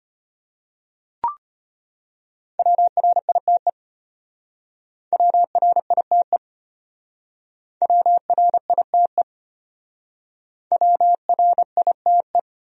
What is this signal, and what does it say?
Morse code, heard as a keyed tone, about 30 wpm, 700 Hz. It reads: WRITE WRITE WRITE WRITE
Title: Speed-Racing Top 200 Words 25wpm